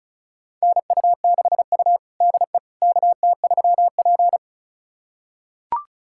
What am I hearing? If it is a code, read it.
Morse code, heard as a keyed tone, about 36 wpm, 700 Hz. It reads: NU6U DE KT3P
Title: USA Callsign DE Callsign - 4 Characters - 35wpm